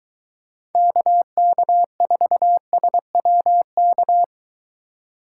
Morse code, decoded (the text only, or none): KK4SWK